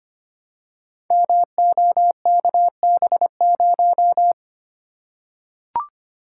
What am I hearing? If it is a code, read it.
Morse code, heard as a keyed tone, about 25 wpm, 700 Hz. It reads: MOKB0
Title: Code Groups - 5 Letter:Numbers - 25wpm